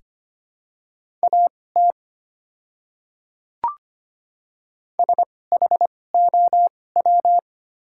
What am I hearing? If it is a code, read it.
Morse code, heard as a keyed tone, about 25 wpm, spaced as for 18 wpm, 700 Hz. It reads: AT SHOW
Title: 2x Character Spacing - Top 500 Words - 25wpm